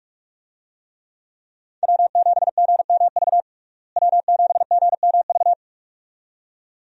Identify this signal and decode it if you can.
Morse code, heard as a keyed tone, about 47 wpm, 700 Hz. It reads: W7GMV W7GMV
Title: USA Call Signs - 5 Characters - Repeated 2x - 45wpm